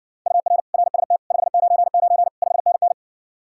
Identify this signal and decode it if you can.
Morse code, heard as a keyed tone, about 60 wpm, 700 Hz. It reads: UR RST 599 5NN